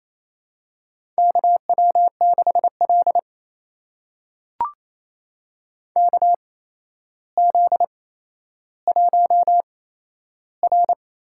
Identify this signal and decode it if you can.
Morse code, heard as a keyed tone, about 27 wpm, spaced as for 12 wpm, 700 Hz. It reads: KW6L KZ1R
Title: USA Call Signs - 4 Characters - 8x Character Spacing - 28wpm